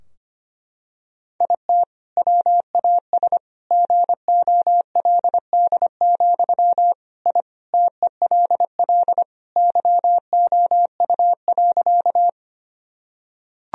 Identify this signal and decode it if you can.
Morse code, heard as a keyed tone, about 25 wpm, 700 Hz. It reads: IT WAS GOLD, I TELL YOU.
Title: Sentences from Top 1000 Words - Review Part 3 - 25wpm